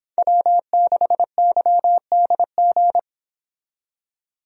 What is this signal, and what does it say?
Morse code, heard as a keyed tone, about 25 wpm, 700 Hz. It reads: W6YDG